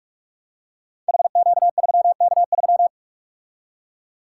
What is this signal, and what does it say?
Morse code, heard as a keyed tone, about 44 wpm, 700 Hz. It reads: HQ3K3
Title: Code Groups - 5 Letter:Numbers - 45wpm